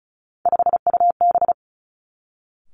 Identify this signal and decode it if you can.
Morse code, heard as a keyed tone, about 36 wpm, 700 Hz. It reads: 5UB
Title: Three Challenging Letter-Number Combinations 35wpm